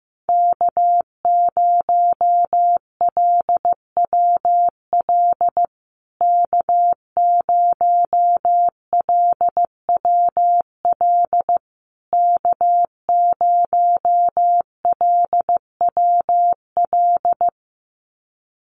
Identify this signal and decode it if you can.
Morse code, heard as a keyed tone, about 15 wpm, 700 Hz. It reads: K0LWL K0LWL K0LWL
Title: USA Call Signs Repeated 3x 15wpm